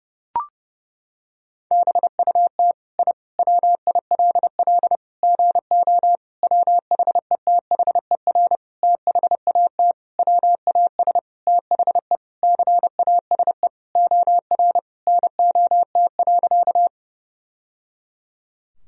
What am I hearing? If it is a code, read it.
Morse code, heard as a keyed tone, about 30 wpm, 700 Hz. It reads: BUT I WILL GO WHETHER THAT WAS THE CASE OR NOT.
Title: Sentences from Top 1000 Words - Review Part 3 - 30wpm